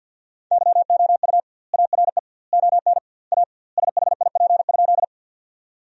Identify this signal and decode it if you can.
Morse code, heard as a keyed tone, about 50 wpm, 700 Hz. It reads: YOU ARE ON A SHIP?
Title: 2x Word-Spacing - Sentences from Top 400 Words - 50wpm